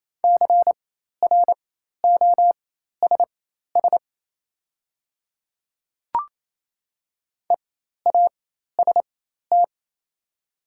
Morse code, heard as a keyed tone, about 28 wpm, spaced as for 13 wpm, 700 Hz. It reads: CROSS EAST
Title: 4x Character Spacing - Top 500 Words - 28wpm